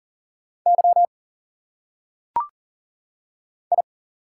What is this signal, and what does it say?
Morse code, heard as a keyed tone, about 40 wpm, 700 Hz. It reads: Y I
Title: Single Letters - Phonetic Pronunciation 40wpm